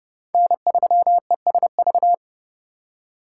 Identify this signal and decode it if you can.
Morse code, heard as a keyed tone, about 36 wpm, 700 Hz. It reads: N3ESV
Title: Speed Racing - USA 1x3 Call Signs 30wpm